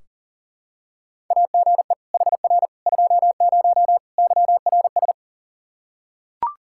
Morse code, heard as a keyed tone, about 40 wpm, 700 Hz. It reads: AGE HR 20 YRS
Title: Practice QSO - Exchange 3 - More Info - 40wpm